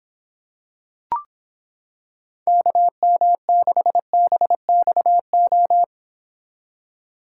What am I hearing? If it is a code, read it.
Morse code, heard as a keyed tone, about 26 wpm, 700 Hz. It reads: KM6BXO